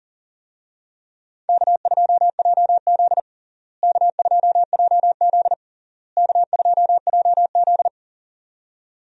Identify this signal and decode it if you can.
Morse code, heard as a keyed tone, about 40 wpm, 700 Hz. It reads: K2JZ K2JZ K2JZ
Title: USA Call Signs - 4 Characters - Repeated 3x - 40wpm